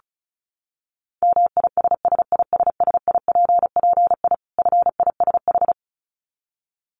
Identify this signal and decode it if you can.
Morse code, heard as a keyed tone, about 34 wpm, 700 Hz. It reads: MISSISSIPPI FISH
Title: Dit Challenge 35wpm